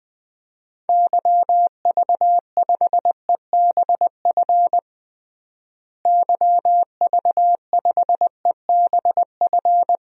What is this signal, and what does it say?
Morse code, heard as a keyed tone, about 21 wpm, 700 Hz. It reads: YV5EBF YV5EBF
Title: DXCC-100 Call Signs - 6 Characters - Repeated 2x - 20wpm